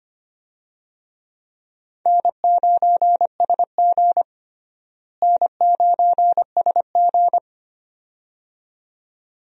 Morse code, heard as a keyed tone, about 25 wpm, 700 Hz. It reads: N9SG N9SG
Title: USA Call Signs - 4 Characters - Repeated 2x - 25wpm